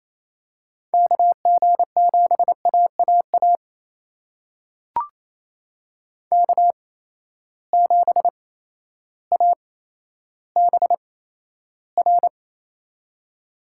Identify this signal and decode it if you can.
Morse code, heard as a keyed tone, about 28 wpm, spaced as for 13 wpm, 700 Hz. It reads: KG7AAA K7ABR
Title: USA Call Signs - 8x Character Spacing - 28wpm